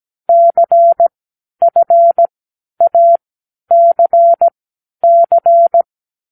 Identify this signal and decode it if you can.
Morse code, heard as a keyed tone, about 17 wpm, spaced as for 10 wpm, 680 Hz. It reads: CFACC
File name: k7qo / Lesson9(A-F)